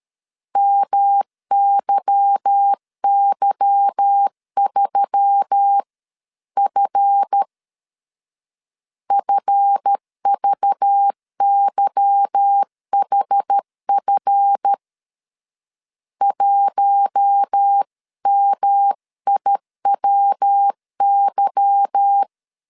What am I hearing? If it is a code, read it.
Morse code, heard as a keyed tone, about 13 wpm, 800 Hz. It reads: MYY3F FVYHF 1MIWY
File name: LEZIONI / 8 YJ45 / 8 Y 12wpm 2228